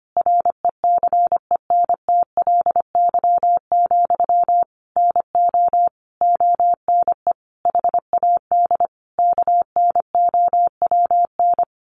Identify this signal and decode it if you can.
Morse code, heard as a keyed tone, about 25 wpm, 700 Hz. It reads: RECENTLY, NO ONE HAD KNOWN
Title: Sentences from Top 1200 Words 25wpm